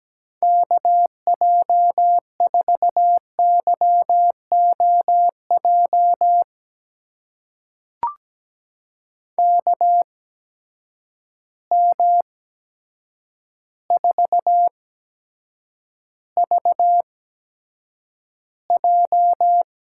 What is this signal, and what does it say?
Morse code, heard as a keyed tone, about 17 wpm, spaced as for 8 wpm, 700 Hz. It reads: KJ4YOJ KM4VJ
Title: USA Call Signs - 6 Characters - 8x Character Spacing - 17wpm